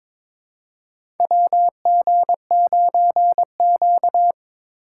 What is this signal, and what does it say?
Morse code, heard as a keyed tone, about 22 wpm, 700 Hz. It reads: WG9Q
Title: USA Call Signs - 4 Characters - 6x Character Spacing - 22wpm